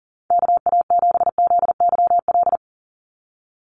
Morse code, heard as a keyed tone, about 40 wpm, 690 Hz. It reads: KA7ZYL
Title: USA Call Signs 40wpm